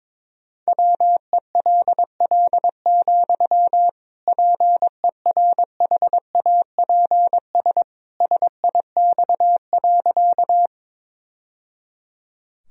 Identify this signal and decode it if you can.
Morse code, heard as a keyed tone, about 22 wpm, 700 Hz. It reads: WELL, PERHAPS SIX.
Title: Sentences from Top 1000 Words - Review Part 2 - 22wpm